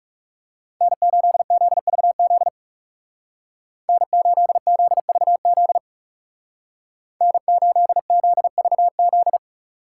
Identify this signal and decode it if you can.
Morse code, heard as a keyed tone, about 39 wpm, 700 Hz. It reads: N8ZVZ N8ZVZ N8ZVZ